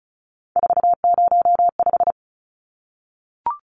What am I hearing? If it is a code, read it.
Morse code, heard as a keyed tone, about 34 wpm, 700 Hz. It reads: 405
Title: Numbers - 3-Digit 35wpm